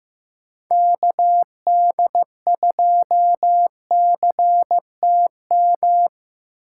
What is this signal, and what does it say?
Morse code, heard as a keyed tone, about 18 wpm, 700 Hz. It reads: KD2CTM
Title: Speed Racing - USA 2x3 Call Signs 15wpm